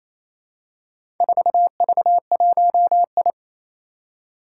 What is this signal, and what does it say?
Morse code, heard as a keyed tone, about 27 wpm, 700 Hz. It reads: 4V1I
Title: Four Challenging Letter-Number Combinations 28wpm